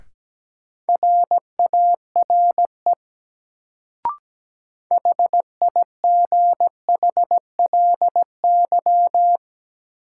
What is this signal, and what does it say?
Morse code, heard as a keyed tone, about 17 wpm, 700 Hz. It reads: RARE HIGHLY
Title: Sentences from Top 1100 Words - New Word Drill Alt - 17wpm